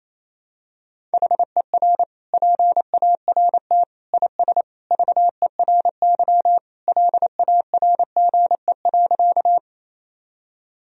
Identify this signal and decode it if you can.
Morse code, heard as a keyed tone, about 28 wpm, 700 Hz. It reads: HER PART IS VERY LARGE.
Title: Sentences from Top 1000 Words - Review Part 1 - 28wpm